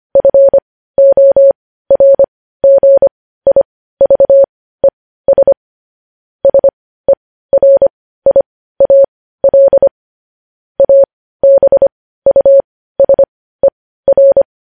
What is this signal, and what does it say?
Morse code, heard as a keyed tone, about 25 wpm, spaced as for 15 wpm, 550 Hz. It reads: FORGIVES SERIAL ABUSER
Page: news-headlines-in-morse-code-at-15-wpm